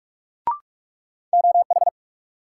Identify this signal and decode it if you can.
Morse code, heard as a keyed tone, about 42 wpm, 700 Hz. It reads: OH
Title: Sentences from Top 500 Words - New Word Drill - 45wpm